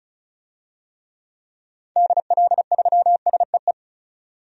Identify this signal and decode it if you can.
Morse code, heard as a keyed tone, about 36 wpm, 700 Hz. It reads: DL3SEE
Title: DXCC-100 Call Signs - 6 Characters - 35wpm